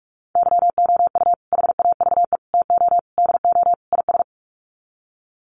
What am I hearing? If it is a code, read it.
Morse code, heard as a keyed tone, about 44 wpm, 700 Hz. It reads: YOU HAVE TO DO IS
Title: Sets of 5 Words 45wpm